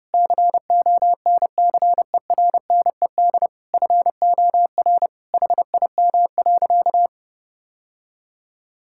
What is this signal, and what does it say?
Morse code, heard as a keyed tone, about 30 wpm, 700 Hz. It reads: CONCERNED FOR HIM.
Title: Sentences from Top 1000 Words - Review Part 6 - 30wpm